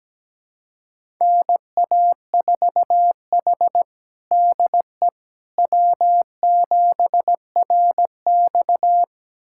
Morse code, heard as a keyed tone, about 17 wpm, 700 Hz. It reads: NA4H DE W7RX
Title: USA Callsign DE Callsign - 4 Characters - 17wpm